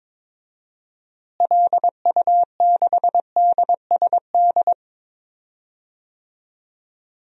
Morse code, heard as a keyed tone, about 22 wpm, 700 Hz. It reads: LU6DSD
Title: DXCC-100 Call Signs - 6 Characters - Repeated 3x - 22wpm